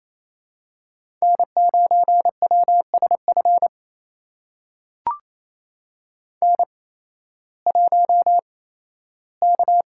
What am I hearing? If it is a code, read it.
Morse code, heard as a keyed tone, about 28 wpm, spaced as for 15 wpm, 700 Hz. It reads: N9WSF N1K
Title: USA Call Signs - 5 Characters - 8x Character Spacing - 28wpm